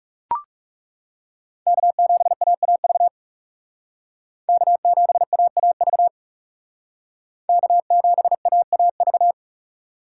Morse code, heard as a keyed tone, about 38 wpm, 700 Hz. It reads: K7AAV K7AAV K7AAV